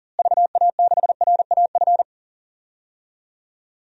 Morse code, heard as a keyed tone, about 40 wpm, 700 Hz. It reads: VA6RAF
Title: DXCC-100 Call Signs - 6 Characters - Repeated 2x - 40wpm